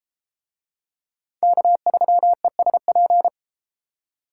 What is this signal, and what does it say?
Morse code, heard as a keyed tone, about 31 wpm, 700 Hz. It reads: K3ESP